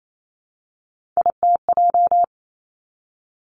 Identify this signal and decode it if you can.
Morse code, heard as a keyed tone, about 28 wpm, 700 Hz. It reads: ITJ
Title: Three Challenging Letter-Number Combinations 28wpm